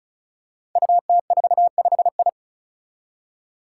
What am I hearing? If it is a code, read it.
Morse code, heard as a keyed tone, about 36 wpm, 700 Hz. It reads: UT45I
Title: Five Challenging Letter-Number Combinations 35wpm